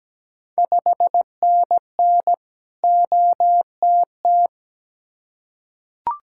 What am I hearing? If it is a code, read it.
Morse code, heard as a keyed tone, about 17 wpm, 690 Hz. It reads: 5NN OTT
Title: Ontario QSO Party Exchange 17wpm